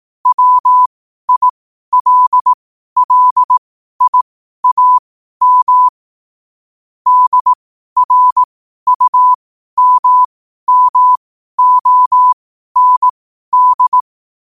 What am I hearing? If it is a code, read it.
Morse code, heard as a keyed tone, about 18 wpm, spaced as for 12 wpm, 1000 Hz. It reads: WILLIAM DRUMMOND
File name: qotdfiles / 12wpm